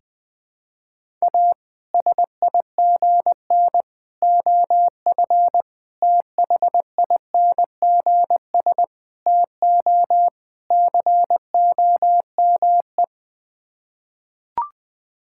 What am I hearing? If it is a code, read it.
Morse code, heard as a keyed tone, about 20 wpm, 700 Hz. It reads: A SIGN OF THINGS TO COME
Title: Sets of 6 Words - 20wpm